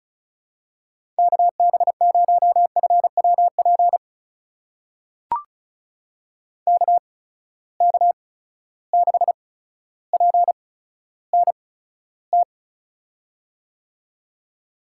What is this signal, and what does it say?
Morse code, heard as a keyed tone, about 36 wpm, spaced as for 15 wpm, 700 Hz. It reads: KB0FWP KK6PNT
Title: USA Call Signs - 6 Characters - 8x Character Spacing - 35wpm